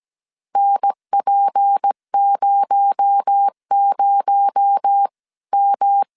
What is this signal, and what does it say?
Morse code, heard as a keyed tone, about 17 wpm, 800 Hz. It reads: NP00M
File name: 16wpm